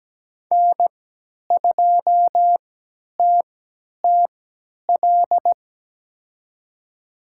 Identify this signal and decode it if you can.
Morse code, heard as a keyed tone, about 17 wpm, spaced as for 10 wpm, 700 Hz. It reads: N2TTL
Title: USA Call Signs - 5 Characters - 3x Character Spacing - 17wpm